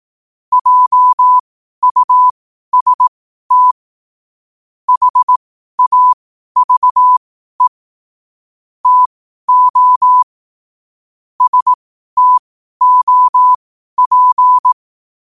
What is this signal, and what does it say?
Morse code, heard as a keyed tone, about 18 wpm, spaced as for 12 wpm, 1000 Hz. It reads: JUST HAVE TO STOP
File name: qotdfiles / 12wpm